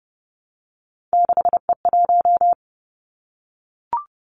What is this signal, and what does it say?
Morse code, heard as a keyed tone, about 30 wpm, 700 Hz. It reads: 6E1
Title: Three Challenging Letter-Number Combinations 30wpm